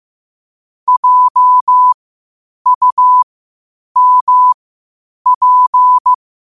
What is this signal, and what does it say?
Morse code, heard as a keyed tone, about 15 wpm, spaced as for 8 wpm, 1000 Hz. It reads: JUMP
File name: qotdfiles / 8wpm